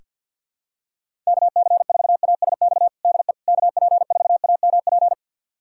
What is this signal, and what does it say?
Morse code, heard as a keyed tone, about 47 wpm, 700 Hz. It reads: KC4ASX DE KP4AMP
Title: USA Callsign DE Callsign 50wpm